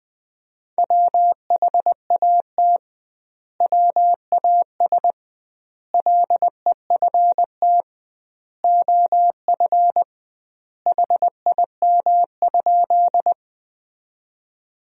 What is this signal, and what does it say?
Morse code, heard as a keyed tone, about 20 wpm, 700 Hz. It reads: WHAT WAS LEFT OF HIM?
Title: 2x Word-Spacing - Sentences from Top 300 Words - 20wpm